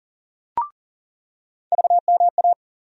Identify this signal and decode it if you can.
Morse code, heard as a keyed tone, about 40 wpm, 700 Hz. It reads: VMA
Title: USA Call Sign Postfixes - 3 Letters 40wpm